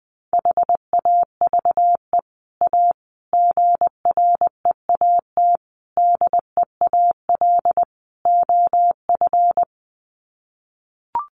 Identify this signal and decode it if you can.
Morse code, heard as a keyed tone, about 20 wpm, 700 Hz. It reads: HAVE A GREAT DEAL OF
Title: Sets of 5 Words Encore2 20wpm